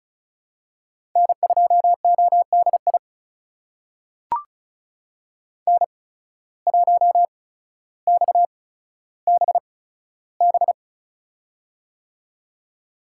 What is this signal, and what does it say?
Morse code, heard as a keyed tone, about 34 wpm, spaced as for 15 wpm, 690 Hz. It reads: N2ODI N1XBB
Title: USA Call Signs - 5 Characters - 8x Character Spacing - 35wpm